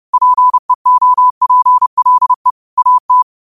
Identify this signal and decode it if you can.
Morse code, heard as a keyed tone, about 30 wpm, 1000 Hz. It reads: PEOPLE AT
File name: qotdfiles / 30wpm